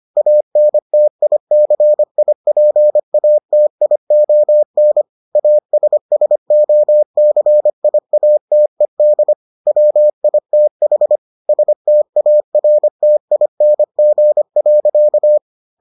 Morse code, heard as a keyed tone, about 25 wpm, 600 Hz. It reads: ANTICIPATION ASSOCIATED WITH STARTING.